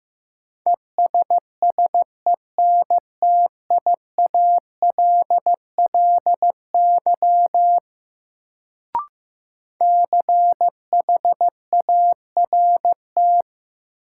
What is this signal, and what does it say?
Morse code, heard as a keyed tone, about 15 wpm, 700 Hz. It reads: ESSENTIALLY CHART
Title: Sentences from Top 1300 Words - New Word Drill Alt - 15wpm